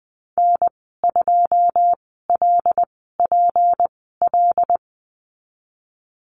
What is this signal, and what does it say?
Morse code, heard as a keyed tone, about 20 wpm, spaced as for 15 wpm, 700 Hz. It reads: N2LPL
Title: 186 Lesson 30g - Letter b - Call Signs 20wpm